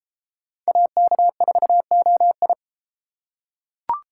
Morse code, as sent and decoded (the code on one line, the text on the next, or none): .- -.- ....- --- ..
AK4OI